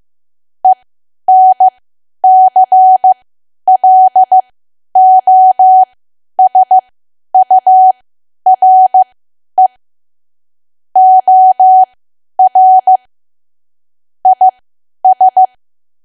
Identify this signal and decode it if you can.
Morse code, heard as a keyed tone, about 15 wpm, spaced as for 10 wpm, 750 Hz. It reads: ENCLOSURE OR IS
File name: Archive / 10 WPM / 10WPM